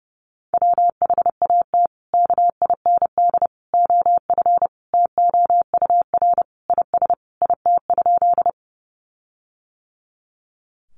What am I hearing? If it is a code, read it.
Morse code, heard as a keyed tone, about 30 wpm, 700 Hz. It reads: WHAT KIND OF TOUR IS IT?
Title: Sentences from Top 1400 Words 30wpm